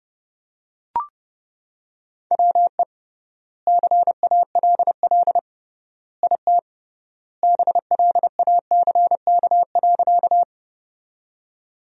Morse code, heard as a keyed tone, about 30 wpm, 700 Hz. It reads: WE CALL IT BLACK.
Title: 3x Word-Spacing - Sentences from Top 400 Words - 30wpm